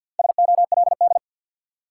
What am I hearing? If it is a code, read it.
Morse code, heard as a keyed tone, about 49 wpm, 700 Hz. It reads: SOLD
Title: Sentences from Top 1400 Words - New Word Drill - 50wpm